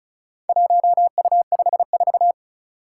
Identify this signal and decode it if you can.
Morse code, heard as a keyed tone, about 36 wpm, 700 Hz. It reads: 1U54
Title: Four Challenging Letter-Number Combinations 35wpm